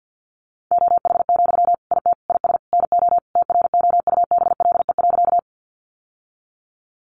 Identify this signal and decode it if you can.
Morse code, heard as a keyed tone, about 50 wpm, 700 Hz. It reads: OH, IT IS NO TROUBLE.
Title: Sentences from Top 1200 Words 50wpm